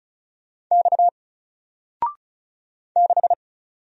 Morse code, heard as a keyed tone, about 36 wpm, 700 Hz. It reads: X 6
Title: Single Character (Letter:Number:Punctuation:Prosign) Alt 35wpm